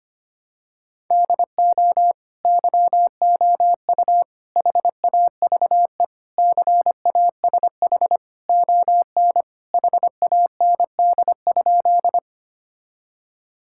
Morse code, heard as a keyed tone, about 25 wpm, 700 Hz. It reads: DO YOU HAVE CASH ON HAND?
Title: Sentences from Top 1000 Words - Review Part 5 - 25wpm